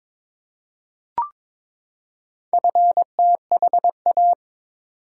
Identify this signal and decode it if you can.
Morse code, heard as a keyed tone, about 23 wpm, 700 Hz. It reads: FTHA
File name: practice / 22wpm / Four Letter-Number Combinations - 22wpm